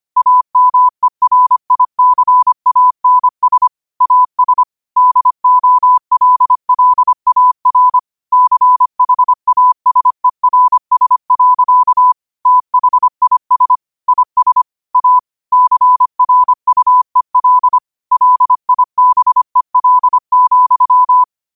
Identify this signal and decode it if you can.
Morse code, heard as a keyed tone, about 25 wpm, 1000 Hz. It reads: AMERICANS AS DOLLAR CHASERS. THIS IS A CRUEL LIBEL,
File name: qotdfiles / 25wpm